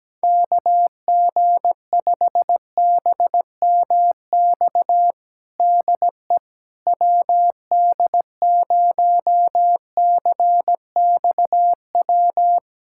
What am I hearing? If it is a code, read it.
Morse code, heard as a keyed tone, about 17 wpm, 700 Hz. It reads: KG5BMX DE WD0CXW
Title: USA Callsign DE Callsign - 6 Characters - 17wpm